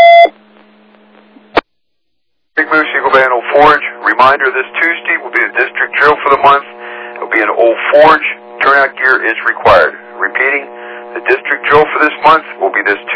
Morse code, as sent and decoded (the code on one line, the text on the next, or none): none